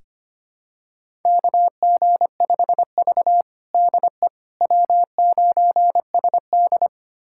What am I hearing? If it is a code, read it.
Morse code, heard as a keyed tone, about 25 wpm, 700 Hz. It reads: KG5V DE W9SD
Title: USA Callsign DE Callsign - 4 Characters - 25wpm